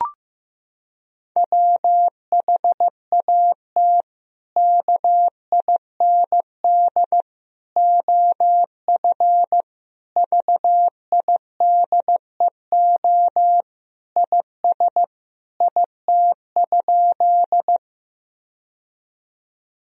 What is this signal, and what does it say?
Morse code, heard as a keyed tone, about 16 wpm, 700 Hz. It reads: WHAT KIND OF VIDEO IS IT?
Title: Sentences from Top 1000 Words - Review Part 4 - 15wpm